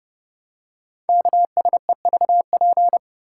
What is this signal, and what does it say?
Morse code, heard as a keyed tone, about 31 wpm, 700 Hz. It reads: KSEVP
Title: Code Groups - 5 Letter:Numbers - 30wpm